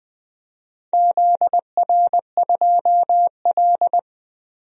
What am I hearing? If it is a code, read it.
Morse code, heard as a keyed tone, about 20 wpm, 700 Hz. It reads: ZR2L
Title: DXCC-100 Call Signs - 4 Characters - Repeated 3x - 20wpm